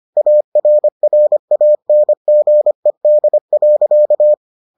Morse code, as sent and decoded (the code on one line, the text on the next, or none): .- .-. .-. .- -. --. -.. .-.-.-
ARRANGED.